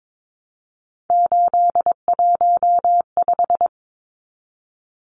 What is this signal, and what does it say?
Morse code, heard as a keyed tone, about 22 wpm, 700 Hz. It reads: 815
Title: Numbers - 3-Digit 22wpm